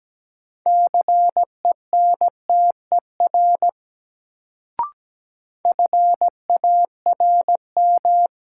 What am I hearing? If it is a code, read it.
Morse code, heard as a keyed tone, about 17 wpm, 700 Hz. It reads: CENTER FARM
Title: Sentences from Top 1000 Words - Review Part 2 - New Word Drill - 17wpm